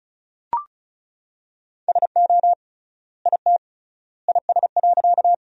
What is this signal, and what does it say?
Morse code, heard as a keyed tone, about 34 wpm, 700 Hz. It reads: SO IT IS.
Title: 3x Word-Spacing - Sentences from Top 100 Words - 35wpm